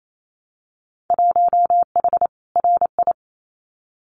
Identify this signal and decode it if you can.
Morse code, heard as a keyed tone, about 28 wpm, 700 Hz. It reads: 1H RI
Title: Winter Field Day Exchanges 28wpm